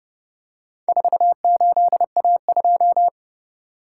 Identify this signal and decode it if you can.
Morse code, heard as a keyed tone, about 28 wpm, 700 Hz. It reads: 48A2